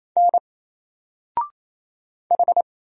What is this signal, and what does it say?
Morse code, heard as a keyed tone, about 27 wpm, 700 Hz. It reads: N H
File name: Single-Letter-Number-28wpm